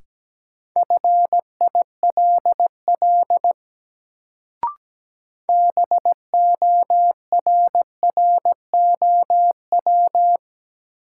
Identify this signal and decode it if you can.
Morse code, heard as a keyed tone, about 17 wpm, 700 Hz. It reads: FILL BORROW
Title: Sentences from Top 1500 Words - New Word Drill Alt - 17wpm